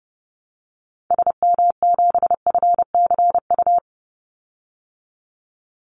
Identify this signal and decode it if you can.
Morse code, heard as a keyed tone, about 30 wpm, 700 Hz. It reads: SM7FCU
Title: Call Signs Repeated 2x - 30wpm